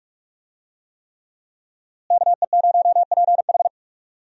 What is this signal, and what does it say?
Morse code, heard as a keyed tone, about 46 wpm, 700 Hz. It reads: KE0PH
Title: USA Call Signs - 5 Characters - Repeated 3x - 45wpm